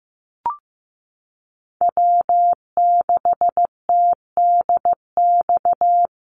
Morse code, heard as a keyed tone, about 15 wpm, 700 Hz. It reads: W6TDX
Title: NA SOTA Chaser Callsigns 15wpm